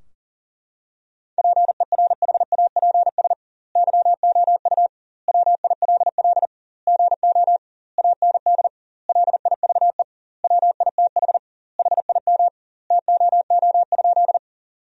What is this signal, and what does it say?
Morse code, heard as a keyed tone, about 40 wpm, 700 Hz. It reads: PERHAPS YOU WILL GO AND LIVE WITH HIM TOO?
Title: 2x Word-Spacing - Sentences from Top 500 Words - 40wpm